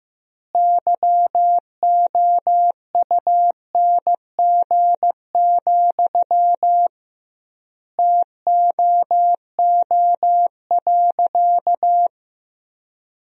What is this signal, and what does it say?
Morse code, heard as a keyed tone, about 15 wpm, 700 Hz. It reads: YOUNG, TOO.
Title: 2x Word-Spacing - Sentences from Top 400 Words - 15wpm